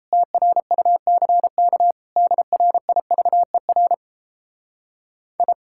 Morse code, heard as a keyed tone, about 34 wpm, 700 Hz. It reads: TRUCK DRIVER I
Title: Speed-Racing Common QSO Phrases - 28wpm